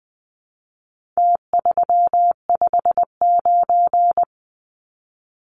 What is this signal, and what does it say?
Morse code, heard as a keyed tone, about 20 wpm, 700 Hz. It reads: T359
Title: Numbers - Contest Style - 4-Digit 20wpm